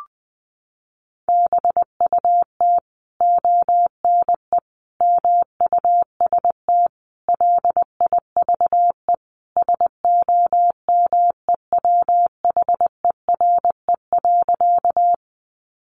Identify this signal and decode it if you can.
Morse code, heard as a keyed tone, about 20 wpm, 700 Hz. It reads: BUT ONE MUST LIVE SOMEWHERE.
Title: Sentences from Top 1400 Words 20wpm